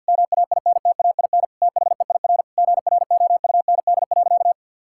Morse code, heard as a keyed tone, about 52 wpm, 700 Hz. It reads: MAINTAIN THEIR GROUND.
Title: Sentences from Top 1000 Words - Review Part 5 - 50wpm